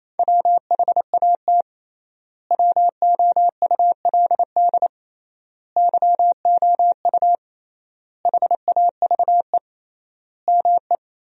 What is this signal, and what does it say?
Morse code, heard as a keyed tone, about 27 wpm, 700 Hz. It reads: WHAT WOULD YOU HAVE ME
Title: 3x Word-Spacing - Sentences from Top 200 Words - 28wpm